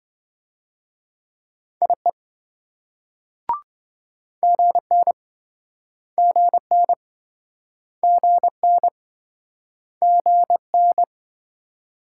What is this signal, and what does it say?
Morse code, heard as a keyed tone, about 26 wpm, 700 Hz. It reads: IE GN GN GN GN